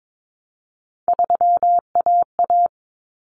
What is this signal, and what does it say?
Morse code, heard as a keyed tone, about 22 wpm, 700 Hz. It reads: 3AA
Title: Common Cut Numbers - 3-Digit 22wpm